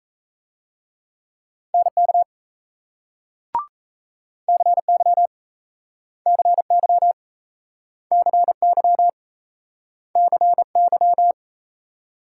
Morse code, heard as a keyed tone, about 38 wpm, 700 Hz. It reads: NK CY CY CY CY